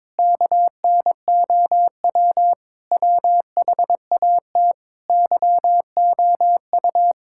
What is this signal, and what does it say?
Morse code, heard as a keyed tone, about 22 wpm, 700 Hz. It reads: KNOW WHAT YOU
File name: practice / 22wpm / Sets of 7 Words - 22wpm